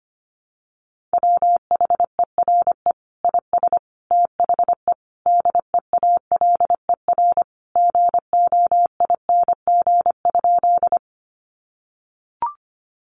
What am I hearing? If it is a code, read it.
Morse code, heard as a keyed tone, about 25 wpm, 700 Hz. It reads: WHERE IS THE DEALER GOING?
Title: Sentences from Top 1100 Words 25wpm